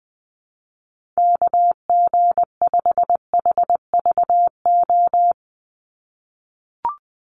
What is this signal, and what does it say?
Morse code, heard as a keyed tone, about 20 wpm, 700 Hz. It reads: KG5HVO
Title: Call Signs - 20wpm